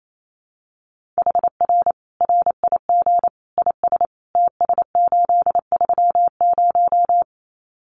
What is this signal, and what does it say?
Morse code, heard as a keyed tone, about 28 wpm, 700 Hz. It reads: HR RIG IS TS830
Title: Common QSO Phrases 28wpm